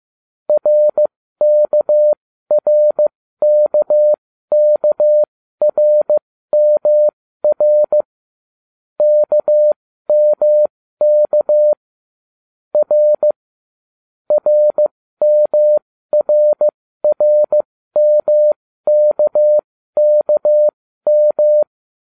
Morse code, heard as a keyed tone, about 15 wpm, spaced as for 13 wpm, 600 Hz. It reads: RKRKKRMR KMK R RMRRMKKM